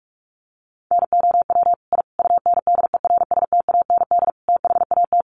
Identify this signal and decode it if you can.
Morse code, heard as a keyed tone, about 44 wpm, 700 Hz. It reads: NOW I UNDERSTAND THAT
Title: Sentences from Top 1000 Words 45wpm